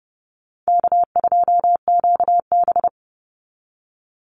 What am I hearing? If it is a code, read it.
Morse code, heard as a keyed tone, about 30 wpm, 700 Hz. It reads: K2QB
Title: Easy Call Signs - 30wpm